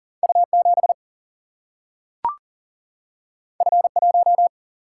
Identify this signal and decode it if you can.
Morse code, heard as a keyed tone, about 40 wpm, 700 Hz. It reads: U 7 F 1
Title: Single-Letter Single-Number 40wpm